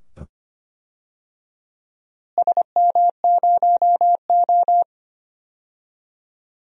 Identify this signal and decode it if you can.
Morse code, heard as a keyed tone, about 25 wpm, 700 Hz. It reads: SM0O
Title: DXCC-100 Call Signs - 4 Characters - Repeated 2x - 25wpm